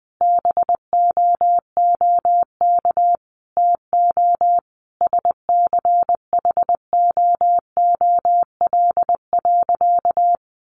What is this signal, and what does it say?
Morse code, heard as a keyed tone, about 20 wpm, 700 Hz. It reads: BOOK TO SCHOOL.
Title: Sentences from Top 1400 Words 20wpm